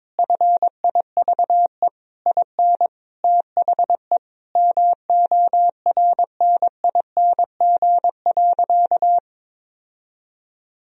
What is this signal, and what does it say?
Morse code, heard as a keyed tone, about 22 wpm, 700 Hz. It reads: FIVE IN THE MORNING.
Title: Sentences from Top 1000 Words - Review Part 2 - 22wpm